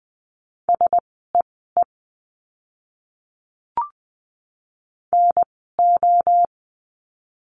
Morse code, heard as a keyed tone, about 20 wpm, spaced as for 14 wpm, 700 Hz. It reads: SEE NO
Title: 061 Lesson 11f - Letter h - Words 20wpm